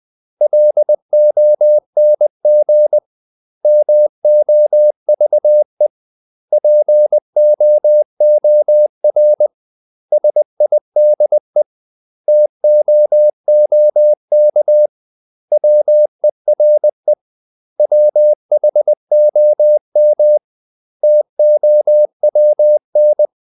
Morse code, heard as a keyed tone, about 20 wpm, 600 Hz. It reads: LONG MOVE POOR SIDE TOOK WERE WHOM TOWN
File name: CW / Practice / 500MostCommonEnglishWords20